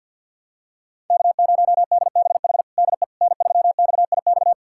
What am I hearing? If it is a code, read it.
Morse code, heard as a keyed tone, about 52 wpm, 700 Hz. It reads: K0DBH DE N3XIX
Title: USA Callsign DE Callsign - 5 Characters - 50wpm